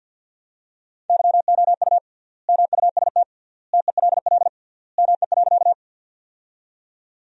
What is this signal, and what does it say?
Morse code, heard as a keyed tone, about 52 wpm, 700 Hz. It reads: YOU MUST TELL ME.
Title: 3x Word-Spacing - Sentences from Top 200 Words - 50wpm